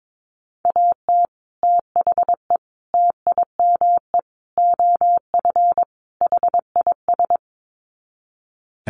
Morse code, heard as a keyed tone, about 22 wpm, 700 Hz. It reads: AT THE TIME OF HIS
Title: Sets of 5 Words Encore2 22wpm